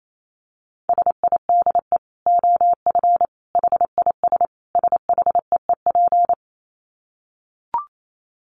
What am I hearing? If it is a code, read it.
Morse code, heard as a keyed tone, about 28 wpm, 700 Hz. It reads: SIDE OF HIS SHEEP
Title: Dit Challenge 28wpm